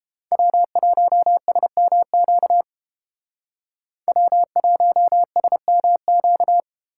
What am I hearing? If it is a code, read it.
Morse code, heard as a keyed tone, about 31 wpm, 700 Hz. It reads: W1SMQ W1SMQ